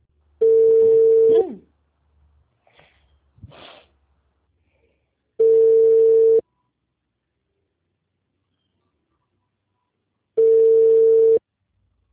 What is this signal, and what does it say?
Ringback tone (China, 450 Hz), with speech.
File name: out- -2015- - - .2